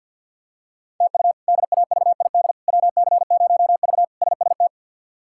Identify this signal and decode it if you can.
Morse code, heard as a keyed tone, about 52 wpm, 700 Hz. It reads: TU DAVID WC0V SST
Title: K1USN SST QSO Practice - 50wpm